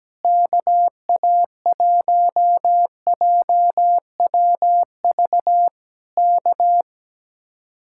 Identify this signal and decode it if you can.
Morse code, heard as a keyed tone, about 17 wpm, 700 Hz. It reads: KA1JWV K
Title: Speed-Racing Common QSO Phrases Encore - 17wpm